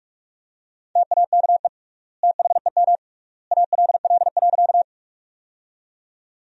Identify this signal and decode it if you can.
Morse code, heard as a keyed tone, about 48 wpm, 700 Hz. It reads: TAKE THEM ALL.
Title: 3x Word-Spacing - Sentences from Top 200 Words - 45wpm